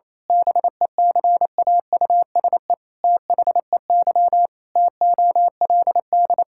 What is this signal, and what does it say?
Morse code, heard as a keyed tone, about 27 wpm, 700 Hz. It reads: BECAUSE THEY TOLD
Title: Pangram Challenge v2 - 28wpm